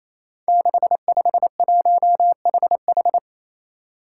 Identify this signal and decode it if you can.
Morse code, heard as a keyed tone, about 29 wpm, 700 Hz. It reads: 651HH
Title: Five Challenging Letter-Number Combinations 28wpm